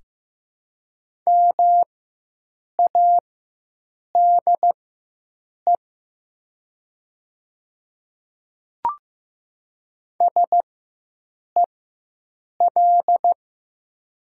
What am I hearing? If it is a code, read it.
Morse code, heard as a keyed tone, about 16 wpm, spaced as for 7 wpm, 700 Hz. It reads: MADE SEL
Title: 4x Character Spacing - Top 200 Words - 15wpm